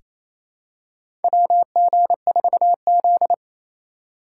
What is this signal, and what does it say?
Morse code, heard as a keyed tone, about 28 wpm, 700 Hz. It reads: WG4Z
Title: USA Call Signs - 4 Characters - 4x Character Spacing - 28wpm